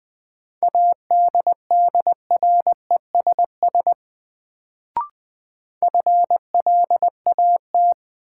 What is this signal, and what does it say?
Morse code, heard as a keyed tone, about 20 wpm, 700 Hz. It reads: ADDRESS FLAT
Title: Sentences from Top 1000 Words - New Word Drill - 20wpm